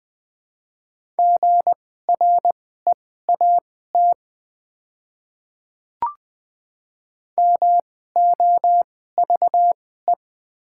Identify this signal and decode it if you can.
Morse code, heard as a keyed tone, about 20 wpm, spaced as for 15 wpm, 700 Hz. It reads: GREAT MOVE
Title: 2x Character Spacing - Top 200 Words - 20wpm